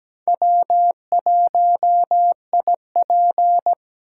Morse code, heard as a keyed tone, about 17 wpm, 700 Hz. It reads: W1IP
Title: USA Call Signs - 4 Characters - 6x Character Spacing - 17wpm